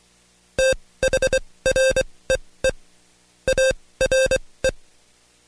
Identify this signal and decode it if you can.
Morse code, heard as a keyed tone, about 24 wpm, spaced as for 18 wpm, 530 Hz. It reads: THREE ARE
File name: Code Quick / AudioDownld2 / Seg135-Stepper8 17WPM